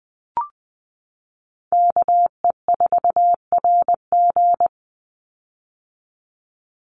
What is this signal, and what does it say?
Morse code, heard as a keyed tone, about 20 wpm, 700 Hz. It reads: KE4RG
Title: Call Signs - 20wpm